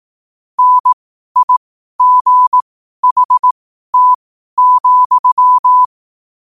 Morse code, heard as a keyed tone, about 18 wpm, spaced as for 12 wpm, 1000 Hz. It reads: NIGHT,
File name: qotdfiles / 12wpm